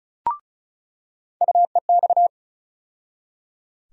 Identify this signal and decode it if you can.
Morse code, heard as a keyed tone, about 36 wpm, 700 Hz. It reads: UEX
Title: USA Call Sign Postfixes - 3 Letters 35wpm